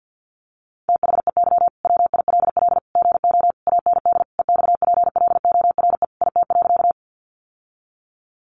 Morse code, heard as a keyed tone, about 50 wpm, 700 Hz. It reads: THEY WILL GO AND EXPLORE IT.
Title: Sentences from Top 1100 Words 50wpm